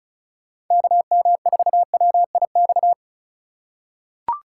Morse code, heard as a keyed tone, about 36 wpm, 700 Hz. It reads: KM4WIX
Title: USA Call Signs - 6 Characters - 4x Character Spacing - 35wpm